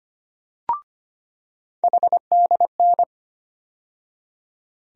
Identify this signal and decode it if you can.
Morse code, heard as a keyed tone, about 25 wpm, 700 Hz. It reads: HDN
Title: USA Call Sign Postfixes - 3 Letters 25wpm